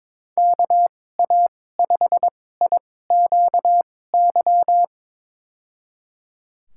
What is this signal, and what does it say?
Morse code, heard as a keyed tone, about 21 wpm, spaced as for 16 wpm, 700 Hz. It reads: KA5IQY
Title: USA Call Signs - 6 Characters - 2x Character Spacing - 22wpm